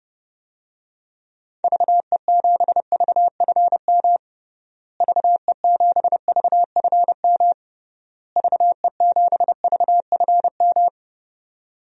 Morse code, heard as a keyed tone, about 29 wpm, 700 Hz. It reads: VE7VFM VE7VFM VE7VFM
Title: DXCC-100 Call Signs - 6 Characters - Repeated 3x - 30wpm